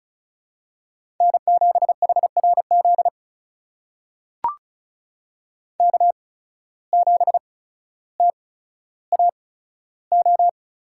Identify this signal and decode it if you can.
Morse code, heard as a keyed tone, about 34 wpm, spaced as for 15 wpm, 700 Hz. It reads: N7HRZ K7TAO
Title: USA Call Signs - 5 Characters - 8x Character Spacing - 35wpm